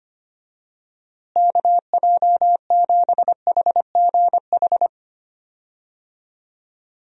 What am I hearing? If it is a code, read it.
Morse code, heard as a keyed tone, about 25 wpm, 700 Hz. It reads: KJ7HGH
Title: USA Call Signs - 6 Characters - Repeated 2x - 25wpm